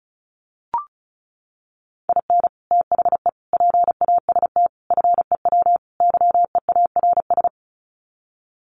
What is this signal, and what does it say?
Morse code, heard as a keyed tone, about 34 wpm, 700 Hz. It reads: IN THE PAST FEW YEARS
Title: Sets of 5 Words 35wpm